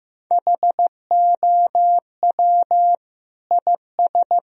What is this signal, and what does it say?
Morse code, heard as a keyed tone, about 16 wpm, 700 Hz. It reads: HOW IS
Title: Sentences from Top 1000 Words - Review Part 3 - 15wpm